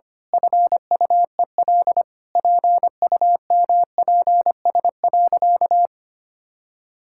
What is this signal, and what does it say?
Morse code, heard as a keyed tone, about 25 wpm, 700 Hz. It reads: FUEL PUMPS.
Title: Pangram Challenge v2 - 25wpm